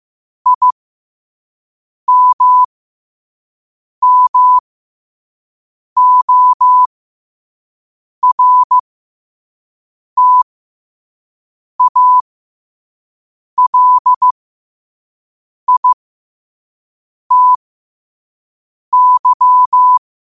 Morse code, heard as a keyed tone, about 15 wpm, spaced as for 5 wpm, 1000 Hz. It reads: IMMORTALITY
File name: qotdfiles / 5wpm